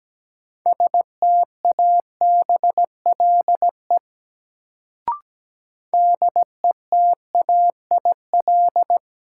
Morse code, heard as a keyed tone, about 17 wpm, 700 Hz. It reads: STABLE DETAIL
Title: Sentences from Top 1500 Words - New Word Drill Alt - 17wpm